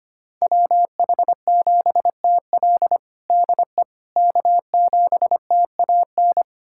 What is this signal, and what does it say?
Morse code, heard as a keyed tone, about 25 wpm, 700 Hz. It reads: WH7TL DE K7TAN
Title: USA Callsign DE Callsign - 5 Characters - 25wpm